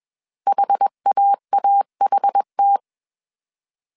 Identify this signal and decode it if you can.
Morse code, heard as a keyed tone, about 21 wpm, 800 Hz. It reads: HAAHT